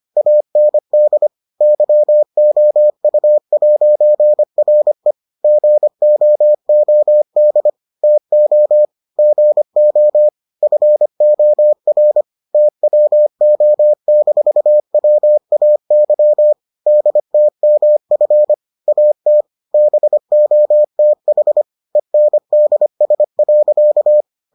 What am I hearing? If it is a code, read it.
Morse code, heard as a keyed tone, about 25 wpm, 600 Hz. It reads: AND YOU'RE GOOD TO GO FOR TWO-WAY DTMF AT BOTH ENDS.